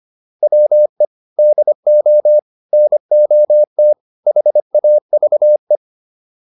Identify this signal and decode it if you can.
Morse code, heard as a keyed tone, about 25 wpm, 600 Hz. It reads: WE DO NOT HAVE